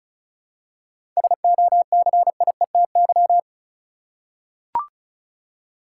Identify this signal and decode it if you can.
Morse code, heard as a keyed tone, about 36 wpm, 700 Hz. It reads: SOCIETY
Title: Seven Letter Words - 35wpm